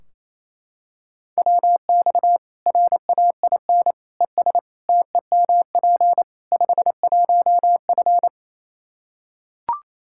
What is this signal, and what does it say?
Morse code, heard as a keyed tone, about 27 wpm, 700 Hz. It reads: WX RAIN ES TEMP 51F
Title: Practice QSO - Exchange 2 - Info - 28wpm